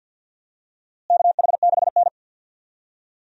Morse code, heard as a keyed tone, about 51 wpm, 700 Hz. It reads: KH6N
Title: DXCC-100 Call Signs - 4 Characters - Repeated 3x - 50wpm